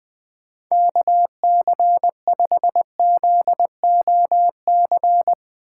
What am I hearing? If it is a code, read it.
Morse code, heard as a keyed tone, about 20 wpm, 700 Hz. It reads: KC5ZOC
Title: Speed Racing - USA Call Signs 17wpm